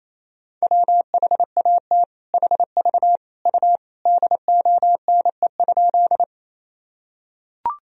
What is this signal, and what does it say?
Morse code, heard as a keyed tone, about 27 wpm, 700 Hz. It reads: WHAT HV U DONE?
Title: Sentences from Top 500 Words and Abbreviations 28wpm